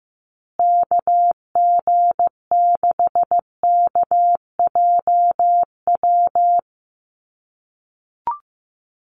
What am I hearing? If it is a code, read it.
Morse code, heard as a keyed tone, about 15 wpm, 700 Hz. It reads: KG6KJW
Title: USA Call Signs Repeated 2x 15wpm